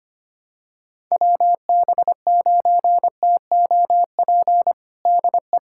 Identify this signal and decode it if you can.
Morse code, heard as a keyed tone, about 25 wpm, 700 Hz. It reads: WB9TOP DE
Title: USA Callsign DE Callsign 25wpm